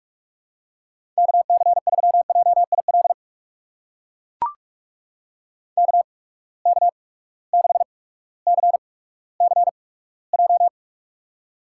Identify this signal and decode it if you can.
Morse code, heard as a keyed tone, about 44 wpm, spaced as for 19 wpm, 700 Hz. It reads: KC3JIL KK6CCJ
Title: USA Call Signs - 6 Characters - 8x Character Spacing - 45wpm